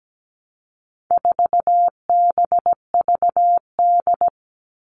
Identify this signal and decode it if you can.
Morse code, heard as a keyed tone, about 17 wpm, 700 Hz. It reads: 4BVD
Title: All Cut Numbers - 4-Digit 17wpm